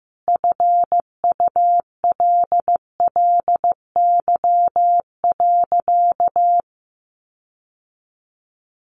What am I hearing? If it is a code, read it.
Morse code, heard as a keyed tone, about 15 wpm, 700 Hz. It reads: FULLY.
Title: Sentences from Top 1300 Words 15wpm